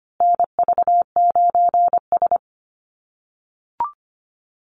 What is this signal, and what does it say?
Morse code, heard as a keyed tone, about 25 wpm, 700 Hz. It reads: NV9S
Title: USA Call Signs Repeated 3x 25wpm